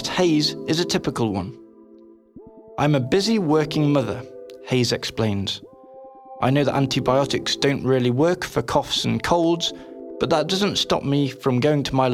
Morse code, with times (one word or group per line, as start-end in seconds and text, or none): none